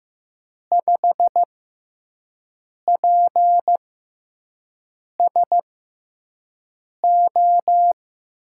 Morse code, heard as a keyed tone, about 15 wpm, spaced as for 5 wpm, 700 Hz. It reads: 5PSO
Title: USA Call Signs - 6 Characters - 6x Character Spacing - 15wpm